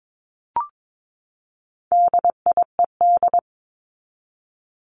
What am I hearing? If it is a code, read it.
Morse code, heard as a keyed tone, about 22 wpm, 700 Hz. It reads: DIED
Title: ED Suffix - Top 100 Words with Base Words 22wpm